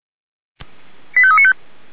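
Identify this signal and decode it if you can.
Five-tone selective call (ZVEI), heard as quick runs of five tones, tones of about 70 ms. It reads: 85395
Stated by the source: ZVEI five-tone call received by SDR.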